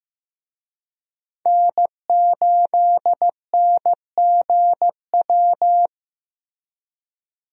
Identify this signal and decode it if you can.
Morse code, heard as a keyed tone, about 15 wpm, 700 Hz. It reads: N8NGW
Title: USA Call Signs - 5 Characters - Repeated 2x - 15wpm